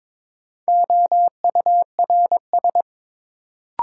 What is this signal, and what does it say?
Morse code, heard as a keyed tone, about 22 wpm, 700 Hz. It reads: OURS
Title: Sentences from Top 1300 Words - New Word Drill Alt - 22wpm